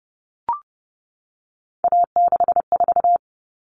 Morse code, heard as a keyed tone, about 30 wpm, 700 Hz. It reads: A64
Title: All Cut Numbers - 3-Digit 30wpm